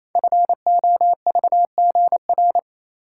Morse code, heard as a keyed tone, about 27 wpm, 700 Hz. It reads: FOVGR
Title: Code Groups - 5 Letters - 28wpm